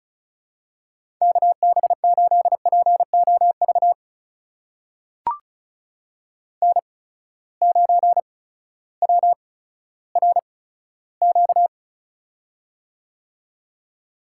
Morse code, heard as a keyed tone, about 36 wpm, spaced as for 16 wpm, 700 Hz. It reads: KB8POV N9WRQ
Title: USA Call Signs - 8x Character Spacing - 35wpm